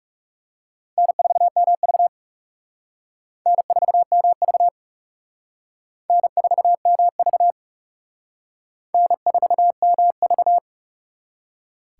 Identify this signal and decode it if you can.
Morse code, heard as a keyed tone, about 37 wpm, 700 Hz. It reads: N4MV N4MV N4MV N4MV
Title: Speed Racing - USA Call Signs 30wpm